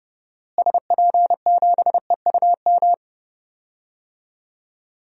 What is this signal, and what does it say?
Morse code, heard as a keyed tone, about 31 wpm, 700 Hz. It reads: SP7EUM
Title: DXCC-100 Call Signs - 6 Characters - Repeated 2x - 30wpm